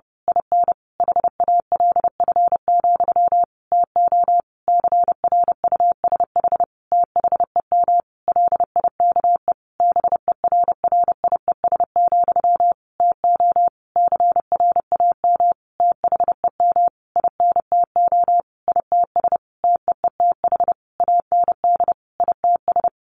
Morse code, heard as a keyed tone, about 30 wpm, 700 Hz. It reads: IN HALF, TO CRUSH THEM LIKE BERRIES, TO CRAM THEM INTO ITS TEETH AND ITS
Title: A Sound Of Thunder 30wpm